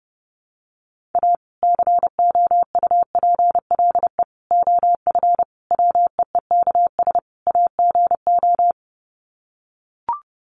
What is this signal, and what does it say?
Morse code, heard as a keyed tone, about 30 wpm, 700 Hz. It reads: A COUPLE OF WEEKS AGO
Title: Sets of 5 Words 30wpm